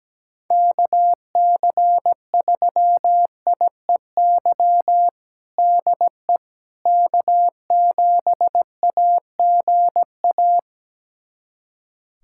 Morse code, heard as a keyed tone, about 17 wpm, 700 Hz. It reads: KC3IEY DE K7AGA
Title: USA Callsign DE Callsign 17wpm